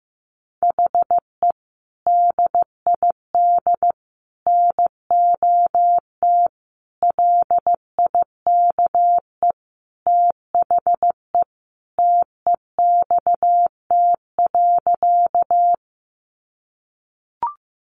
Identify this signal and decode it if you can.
Morse code, heard as a keyed tone, about 15 wpm, 700 Hz. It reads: HE DID NOT LIKE THE TEXT.
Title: Sentences from Top 1200 Words 15wpm